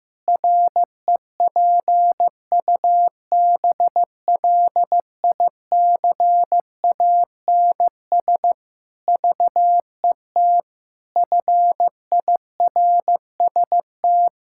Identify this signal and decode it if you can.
Morse code, heard as a keyed tone, about 15 wpm, 700 Hz. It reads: REPUBLICANS VET FIRST
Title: Pangram Challenge v2 - 15wpm